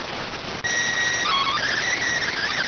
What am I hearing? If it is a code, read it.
SSTV (slow-scan television), PD 290.